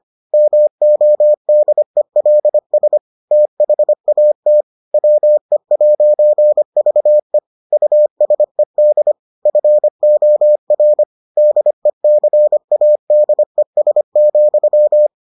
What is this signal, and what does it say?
Morse code, heard as a keyed tone, about 25 wpm, 600 Hz. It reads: MODELS THAT WE'VE USED FOR DECADES,